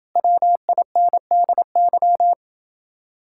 Morse code, heard as a keyed tone, about 28 wpm, 700 Hz. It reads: WINDY